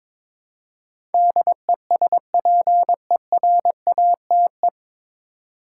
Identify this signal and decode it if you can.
Morse code, heard as a keyed tone, about 22 wpm, 700 Hz. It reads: DESPERATE
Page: Morse Code Ninja practice 22wpm